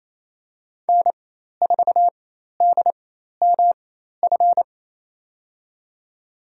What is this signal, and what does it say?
Morse code, heard as a keyed tone, about 28 wpm, spaced as for 13 wpm, 700 Hz. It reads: N4DMF
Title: USA Call Signs - 5 Characters - 4x Character Spacing - 28wpm